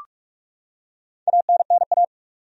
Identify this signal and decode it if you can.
Morse code, heard as a keyed tone, about 42 wpm, 700 Hz. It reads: ANNA
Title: USA Common Names - Max 5 Letters - 45wpm